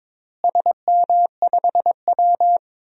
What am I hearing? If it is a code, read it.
Morse code, heard as a keyed tone, about 22 wpm, 700 Hz. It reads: SM5W
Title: DXCC-100 Call Signs - 4 Characters - Repeated 3x - 22wpm